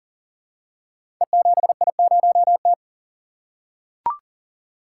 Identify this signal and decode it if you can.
Morse code, heard as a keyed tone, about 42 wpm, 700 Hz. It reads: E7I0T
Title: Code Groups - 5 Letter:Numbers - 40wpm